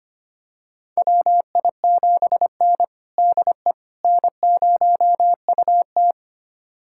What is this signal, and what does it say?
Morse code, heard as a keyed tone, about 25 wpm, 700 Hz. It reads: WI7N DE N0UT
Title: USA Callsign DE Callsign - 4 Characters - 25wpm